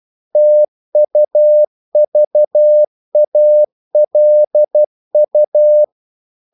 Morse code, heard as a keyed tone, about 12 wpm, 600 Hz. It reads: TUVALU
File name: CW / Practice / NationalCapitals12